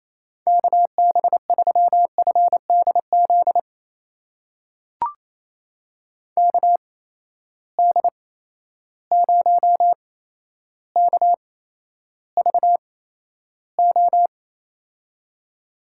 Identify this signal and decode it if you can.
Morse code, heard as a keyed tone, about 27 wpm, spaced as for 12 wpm, 700 Hz. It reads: KB3FDZ KD0KVO
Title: USA Call Signs - 6 Characters - 8x Character Spacing - 28wpm